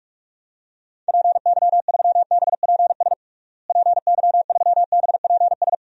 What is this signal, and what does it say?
Morse code, heard as a keyed tone, about 46 wpm, 700 Hz. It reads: PY3BPS PY3BPS
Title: DXCC-100 Call Signs - 6 Characters - Repeated 2x - 45wpm